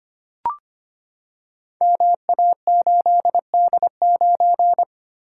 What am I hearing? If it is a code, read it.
Morse code, heard as a keyed tone, about 25 wpm, 700 Hz. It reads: MA8D9
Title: Code Groups - 5 Letter:Numbers - 25wpm